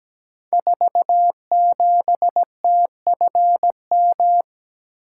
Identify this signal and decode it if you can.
Morse code, heard as a keyed tone, about 17 wpm, 700 Hz. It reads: 47TFM
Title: Code Groups - 5 Letter:Numbers - 17wpm